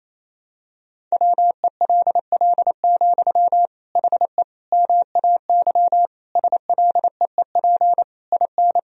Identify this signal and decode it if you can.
Morse code, heard as a keyed tone, about 28 wpm, 700 Hz. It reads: WELL, HE MAY SLEEP IN
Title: Sentences from Top 1000 Words - Review Part 4 - 28wpm